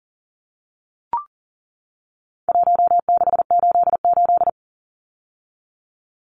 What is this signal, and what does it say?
Morse code, heard as a keyed tone, about 40 wpm, 700 Hz. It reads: 1688
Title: Numbers - Contest Style - 4-Digit 40wpm